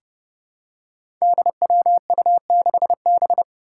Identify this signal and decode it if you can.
Morse code, heard as a keyed tone, about 30 wpm, 700 Hz. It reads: DWU6B
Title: Five Challenging Letter-Number Combinations 30wpm